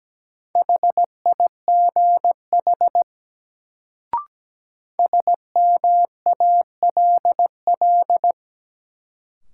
Morse code, heard as a keyed tone, about 17 wpm, 700 Hz. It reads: HIGH SMALL
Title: Sentences from Top 200 Words - New Word Drill - 17wpm